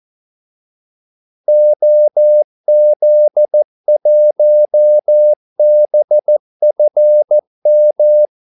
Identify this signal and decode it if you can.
Morse code, heard as a keyed tone, about 14 wpm, 600 Hz. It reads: OZ1BFM
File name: Callsigns-14wpm